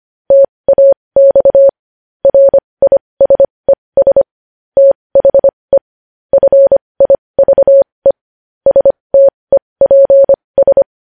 Morse code, heard as a keyed tone, about 25 wpm, spaced as for 20 wpm, 550 Hz. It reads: TAX RISES THE FIVE STEPS
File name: podcast / News headlines-1x-25x20